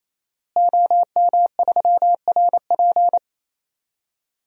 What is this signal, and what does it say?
Morse code, heard as a keyed tone, about 28 wpm, 700 Hz. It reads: OM3RP
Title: DXCC-100 Call Signs - 5 Characters - Repeated 3x - 28wpm